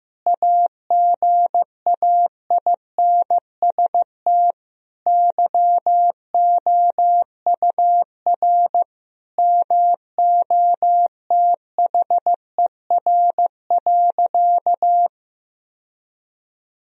Morse code, heard as a keyed tone, about 15 wpm, 700 Hz. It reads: AGAINST YOUR MOTHER.
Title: Sentences from Top 1000 Words - Review Part 2 - 15wpm